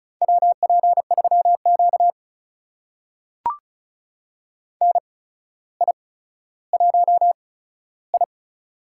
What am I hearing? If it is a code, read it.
Morse code, heard as a keyed tone, about 36 wpm, spaced as for 15 wpm, 700 Hz. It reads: WP3Q NI1I
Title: USA Call Signs - 4 Characters - 8x Character Spacing - 35wpm